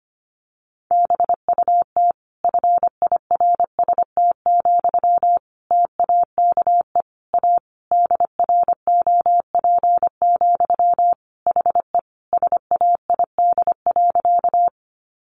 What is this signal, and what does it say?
Morse code, heard as a keyed tone, about 25 wpm, 700 Hz. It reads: BUT FIRST, TAKE A DROP, HE SAID.
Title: Sentences from Top 1400 Words 25wpm